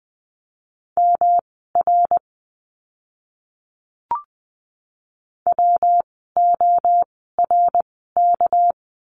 Morse code, heard as a keyed tone, about 20 wpm, spaced as for 14 wpm, 700 Hz. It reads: MR WORK
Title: 244 Lesson 39f - Letter z - Words 20wpm